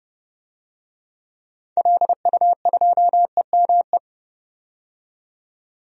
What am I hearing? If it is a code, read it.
Morse code, heard as a keyed tone, about 30 wpm, 700 Hz. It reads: LU2EME
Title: DXCC-100 Call Signs - 6 Characters - 30wpm